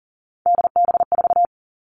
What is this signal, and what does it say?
Morse code, heard as a keyed tone, about 40 wpm, 700 Hz. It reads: DB4
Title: All Cut Numbers - 3-Digit 40wpm